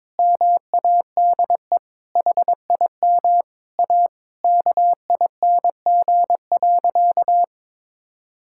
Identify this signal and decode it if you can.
Morse code, heard as a keyed tone, about 22 wpm, 700 Hz. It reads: MADE HIM A KING.
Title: Sentences from Top 1000 Words - Review Part 5 - 22wpm